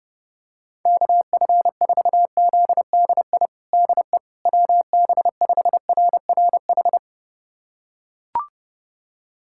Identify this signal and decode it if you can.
Morse code, heard as a keyed tone, about 30 wpm, 700 Hz. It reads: KF4ZDI DE WB5RRH
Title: USA Callsign DE Callsign - 6 Characters - 30wpm